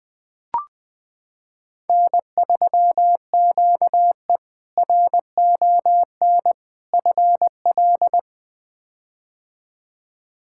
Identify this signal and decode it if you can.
Morse code, heard as a keyed tone, about 20 wpm, 700 Hz. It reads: N3QE RON FL
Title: K1USN SST QSO Practice - 20wpm